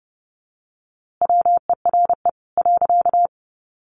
Morse code, heard as a keyed tone, about 30 wpm, 700 Hz. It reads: WERE .
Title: A Sound Of Thunder 30wpm